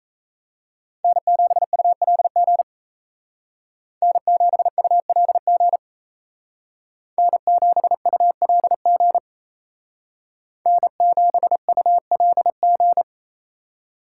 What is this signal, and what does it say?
Morse code, heard as a keyed tone, about 35 wpm, 700 Hz. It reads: N7ULG N7ULG N7ULG N7ULG